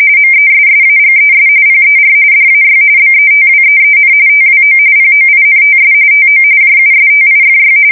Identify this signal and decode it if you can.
RTTY, about 45.45 baud. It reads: MESSAGES PER DAY BETWEEN TWO POINTS TO JUSTIFY
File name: fsk170 / printing-telegraph